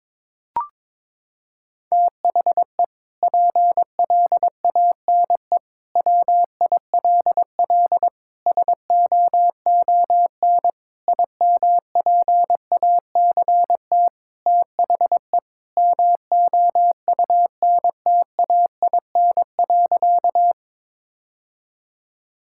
Morse code, heard as a keyed tone, about 22 wpm, 700 Hz. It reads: THE PLANE WILL SOON IMPACT THE MOUNTAIN.
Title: Sentences from Top 1000 Words - Review Part 6 - 22wpm